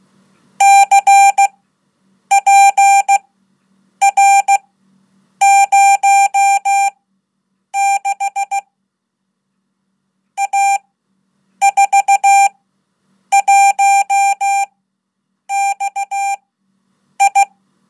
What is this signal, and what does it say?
Morse code, heard as a keyed tone, about 16 wpm, spaced as for 8 wpm, 790 Hz. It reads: CPR06 A41XI
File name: Code / 5min-1